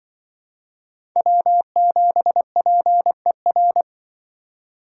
Morse code, heard as a keyed tone, about 25 wpm, 700 Hz. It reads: W7PER